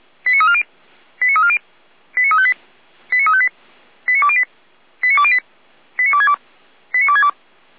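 Five-tone selective call (ZVEI), heard as quick runs of five tones, tones of about 70 ms. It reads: 89340 89340 89358 89358 89208 89208 89262 89262
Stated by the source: Five-tone ZVEI selective call 89110, received by SDR.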